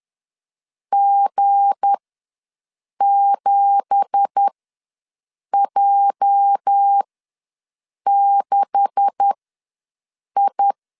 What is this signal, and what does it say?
Morse code, heard as a keyed tone, about 10 wpm, spaced as for 6 wpm, 800 Hz. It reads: G7J6I
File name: LEZIONI / 10 SIGNS / 10B INT ws 10wpm 4134